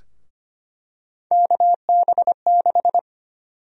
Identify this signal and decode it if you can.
Morse code, heard as a keyed tone, about 25 wpm, 700 Hz. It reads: KB6
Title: USA Call Sign Prefixes - 25wpm